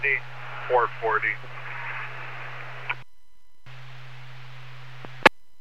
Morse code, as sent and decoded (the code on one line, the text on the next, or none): none